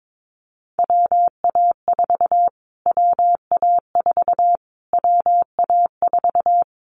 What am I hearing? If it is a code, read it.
Morse code, heard as a keyed tone, about 22 wpm, 700 Hz. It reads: WA4 WA4 WA4
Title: Callsign Prefixes for Top 25 DXCC - Repeated 3x - 22wpm